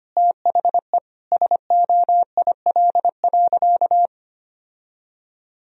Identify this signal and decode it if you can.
Morse code, heard as a keyed tone, about 25 wpm, 700 Hz. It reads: THE SOIL.
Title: Sentences from Top 1000 Words - Review Part 4 - 25wpm